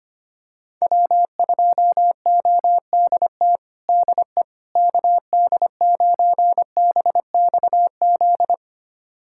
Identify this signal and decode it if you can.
Morse code, heard as a keyed tone, about 25 wpm, 700 Hz. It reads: W2ODT DE KD9BXZ
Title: USA Callsign DE Callsign 25wpm